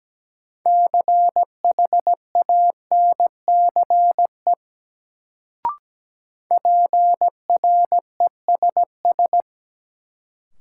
Morse code, heard as a keyed tone, about 17 wpm, 700 Hz. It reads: CHANCE PRESS
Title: Sentences from Top 800 Words - New Word Drill - 17wpm